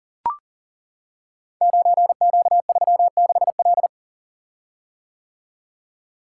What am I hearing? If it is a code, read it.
Morse code, heard as a keyed tone, about 40 wpm, 690 Hz. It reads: 9Q36L
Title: Code Groups - 5 Letter:Numbers - 40wpm